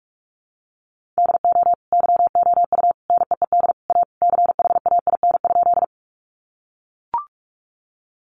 Morse code, heard as a keyed tone, about 44 wpm, 700 Hz. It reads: DO YOU NEED A CHAIN?
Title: Sentences from Top 1500 Words 45wpm